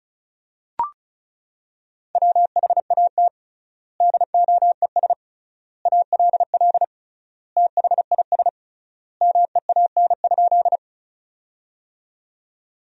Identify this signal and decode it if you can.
Morse code, heard as a keyed tone, about 36 wpm, 700 Hz. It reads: WHAT DOES ALL THIS MEAN?
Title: 3x Word-Spacing - Sentences from Top 200 Words - 35wpm